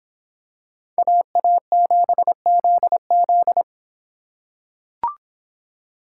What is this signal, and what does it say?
Morse code, heard as a keyed tone, about 25 wpm, 700 Hz. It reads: AA7ZZ